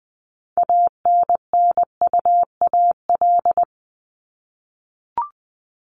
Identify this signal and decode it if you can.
Morse code, heard as a keyed tone, about 20 wpm, 700 Hz. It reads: ANNUAL
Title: Top 1251-1500 Words 20wpm